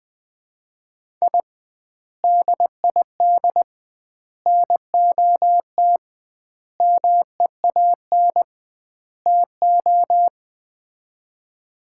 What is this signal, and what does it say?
Morse code, heard as a keyed tone, about 21 wpm, 700 Hz. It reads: I DID NOT MEAN TO
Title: 2x Word-Spacing - Sets of 5 Words - 20wpm